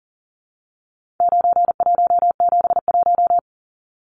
Morse code, heard as a keyed tone, about 40 wpm, 700 Hz. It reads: 9171
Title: Numbers - 4-Digit 40wpm